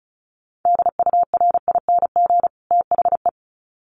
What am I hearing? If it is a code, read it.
Morse code, heard as a keyed tone, about 34 wpm, 700 Hz. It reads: DURING THE
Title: Sets of 2 Words 35wpm